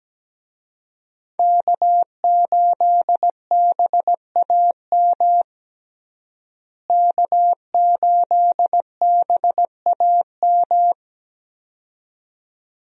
Morse code, heard as a keyed tone, about 17 wpm, 700 Hz. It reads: K8BAM K8BAM
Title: USA Call Signs - 5 Characters - Repeated 2x - 17wpm